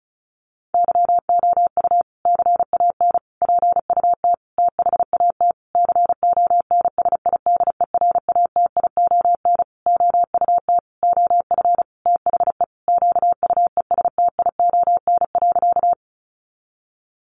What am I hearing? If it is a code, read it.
Morse code, heard as a keyed tone, about 34 wpm, 700 Hz. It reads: YOU CAN PUT THAT CONSIDERATION OUT OF THE QUESTION.
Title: Sentences from Top 1500 Words 35wpm